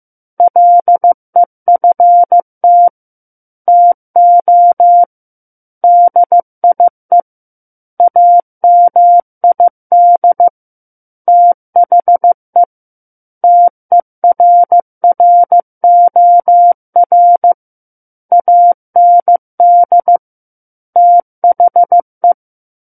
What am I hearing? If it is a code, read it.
Morse code, heard as a keyed tone, about 15 wpm, 700 Hz. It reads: LEFT TO DIE AMID THE TERROR AND THE
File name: Chapter 6 - 15 WPM